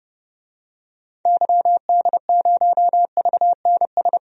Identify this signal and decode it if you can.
Morse code, heard as a keyed tone, about 30 wpm, 700 Hz. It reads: YD0VNS
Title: DXCC-100 Call Signs - 6 Characters - 30wpm